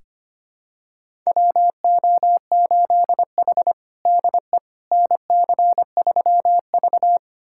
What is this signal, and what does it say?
Morse code, heard as a keyed tone, about 25 wpm, 700 Hz. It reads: WO8H DE NC3V
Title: USA Callsign DE Callsign - 4 Characters - 25wpm